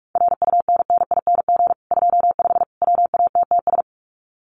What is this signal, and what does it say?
Morse code, heard as a keyed tone, about 44 wpm, 700 Hz. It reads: RUNNING 25 WATTS
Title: Common QSO Phrases 45wpm